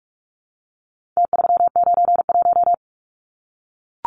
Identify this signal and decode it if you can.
Morse code, heard as a keyed tone, about 44 wpm, 700 Hz. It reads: T391
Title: Numbers - Contest Style - 4-Digit 45wpm